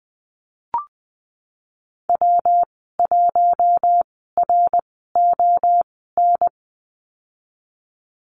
Morse code, heard as a keyed tone, about 20 wpm, spaced as for 15 wpm, 700 Hz. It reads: W1RON
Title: 173 Lesson 28g - Number 9 - Call Signs 20wpm